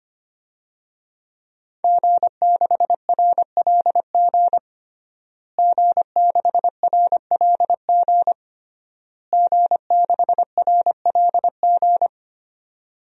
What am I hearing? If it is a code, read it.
Morse code, heard as a keyed tone, about 25 wpm, 700 Hz. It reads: G6RLG G6RLG G6RLG
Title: DXCC-100 Call Signs - 5 Characters - Repeated 3x - 25wpm